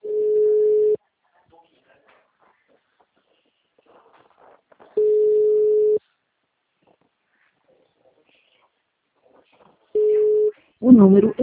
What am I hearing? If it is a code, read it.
Ringback tone (Europe, 425 Hz), with speech.